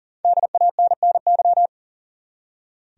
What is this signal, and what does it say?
Morse code, heard as a keyed tone, about 40 wpm, 700 Hz. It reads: DANNY
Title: USA Common Names - 40wpm